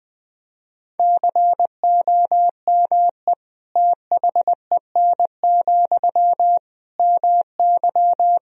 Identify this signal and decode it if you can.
Morse code, heard as a keyed tone, about 21 wpm, 700 Hz. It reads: COME THEN, MY
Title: Sentences from Top 1000 Words - Review Part 2 - 20wpm